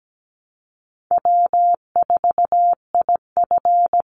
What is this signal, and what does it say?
Morse code, heard as a keyed tone, about 17 wpm, 700 Hz. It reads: W4IF
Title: Call Signs Repeated 2x - 17wpm